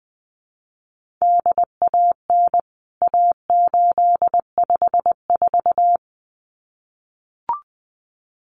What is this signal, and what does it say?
Morse code, heard as a keyed tone, about 20 wpm, 700 Hz. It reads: DAN A854
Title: CWT Exchange 20wpm